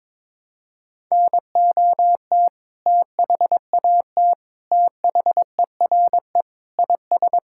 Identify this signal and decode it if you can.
Morse code, heard as a keyed tone, about 22 wpm, 700 Hz. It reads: NOT THAT THERE IS
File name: Sentences with 7-Word Phrases - 22wpm